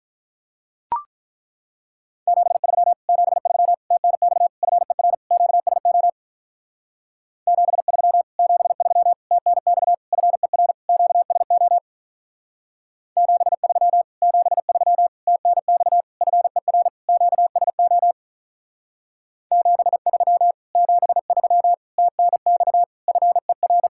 Morse code, heard as a keyed tone, about 42 wpm, 700 Hz. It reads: 73 73 TNX FER QSO 73 73 TNX FER QSO 73 73 TNX FER QSO 73 73 TNX FER
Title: Speed-Racing Common QSO Phrases Encore - 35wpm